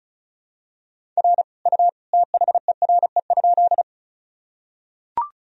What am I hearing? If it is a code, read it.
Morse code, heard as a keyed tone, about 36 wpm, 700 Hz. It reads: R U THERE?
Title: Sentences from Top 100 Words and Abbreviations 35wpm